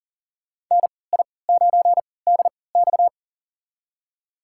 Morse code, heard as a keyed tone, about 38 wpm, spaced as for 22 wpm, 700 Hz. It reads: NI9DX
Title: USA Call Signs - 5 Characters - 3x Character Spacing - 40wpm